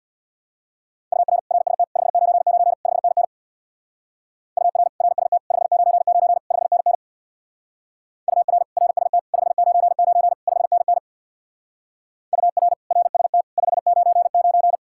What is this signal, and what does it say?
Morse code, heard as a keyed tone, about 64 wpm, 700 Hz. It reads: UR RST 599 5NN UR RST 599 5NN UR RST 599 5NN UR RST 599